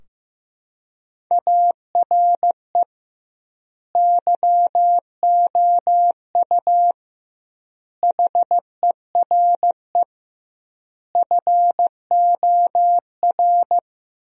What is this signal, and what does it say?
Morse code, heard as a keyed tone, about 16 wpm, 700 Hz. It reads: ARE YOU HERE FOR
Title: 2x Word-Spacing - Sentences from Top 200 Words - 15wpm